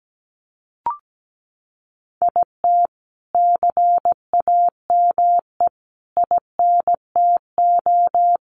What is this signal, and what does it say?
Morse code, heard as a keyed tone, about 17 wpm, 700 Hz. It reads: IT CAME INTO
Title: Sentences from Top 1500 Words 17wpm